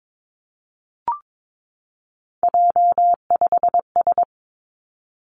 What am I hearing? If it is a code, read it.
Morse code, heard as a keyed tone, about 22 wpm, 700 Hz. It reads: J5S
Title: Three Challenging Letter-Number Combinations 22wpm